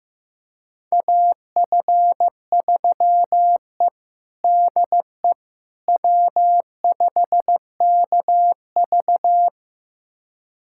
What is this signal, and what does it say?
Morse code, heard as a keyed tone, about 15 wpm, 700 Hz. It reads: AF3E DE W5KV
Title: USA Callsign DE Callsign - 4 Characters - 15wpm